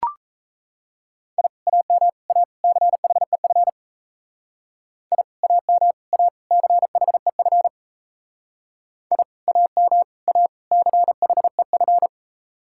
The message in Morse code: I AM A CHEF I AM A CHEF I AM A CHEF

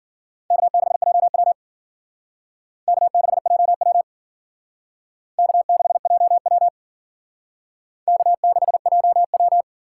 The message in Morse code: K6JW K6JW K6JW K6JW

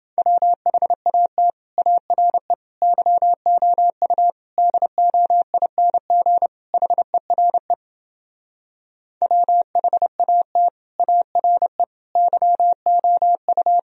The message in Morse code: WHAT ARE YOU DOING HERE WHAT ARE YOU

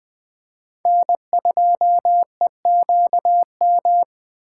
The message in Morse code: N2EQM

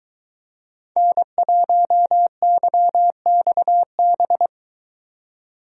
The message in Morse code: N1YXB

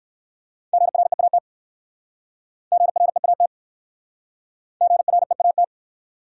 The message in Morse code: GREAT GREAT GREAT

